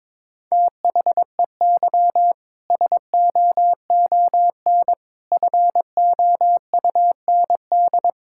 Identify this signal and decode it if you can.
Morse code, heard as a keyed tone, about 22 wpm, 700 Hz. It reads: THEY SOON FOUND
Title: Sentences from Top 1000 Words - Review Part 5 - 22wpm